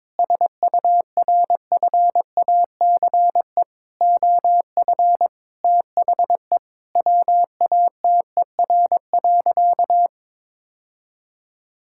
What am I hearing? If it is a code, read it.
Morse code, heard as a keyed tone, about 22 wpm, 700 Hz. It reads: SURFACE OF THE WATER.
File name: practice / 22wpm / Sentences with 4-Word Phrases - 22wpm